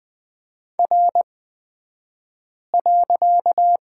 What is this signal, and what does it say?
Morse code, heard as a keyed tone, about 20 wpm, spaced as for 5 wpm, 700 Hz. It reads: R.
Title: 254 Lesson 41d - Period - Two Characters 20wpm